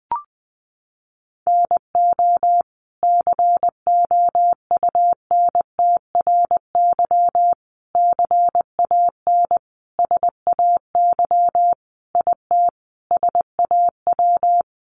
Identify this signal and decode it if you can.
Morse code, heard as a keyed tone, about 20 wpm, 700 Hz. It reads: NO COUNTRY CAN SAY IT SAW
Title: Sentences from Top 800 Words 20wpm